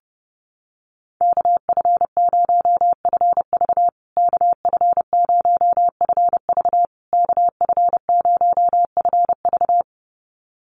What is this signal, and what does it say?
Morse code, heard as a keyed tone, about 30 wpm, 700 Hz. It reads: KF0FV KF0FV KF0FV
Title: USA Call Signs Repeated 3x 30wpm